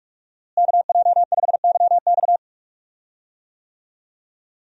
Morse code, heard as a keyed tone, about 46 wpm, 700 Hz. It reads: KJ5YX
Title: USA Call Signs - 5 Characters - Repeated 2x - 45wpm